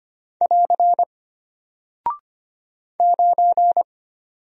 Morse code, heard as a keyed tone, about 25 wpm, 700 Hz. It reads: <AR> 9
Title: Single Character (Letter:Number:Punctuation:Prosign) Alt 25wpm